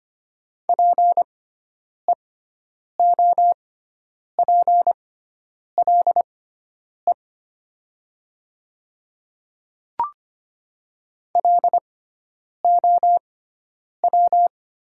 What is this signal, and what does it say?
Morse code, heard as a keyed tone, about 25 wpm, spaced as for 9 wpm, 700 Hz. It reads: PEOPLE LOW
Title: 6x Character Spacing - Top 200 Words - 25wpm